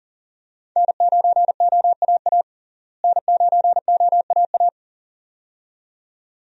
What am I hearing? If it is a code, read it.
Morse code, heard as a keyed tone, about 40 wpm, 700 Hz. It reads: N9OAA N9OAA
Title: USA Call Signs - 5 Characters - Repeated 2x - 40wpm